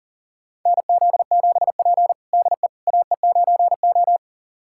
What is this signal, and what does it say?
Morse code, heard as a keyed tone, about 40 wpm, 700 Hz. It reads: NZ7P DE AE9O
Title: USA Callsign DE Callsign - 4 Characters - 40wpm